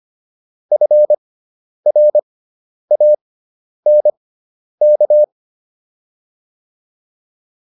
Morse code, heard as a keyed tone, about 25 wpm, spaced as for 10 wpm, 600 Hz. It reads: FRANK
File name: QSO208 10